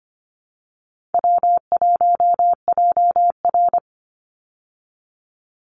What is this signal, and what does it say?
Morse code, heard as a keyed tone, about 24 wpm, 700 Hz. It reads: W1JR
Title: Call Signs Repeated 2x - 25wpm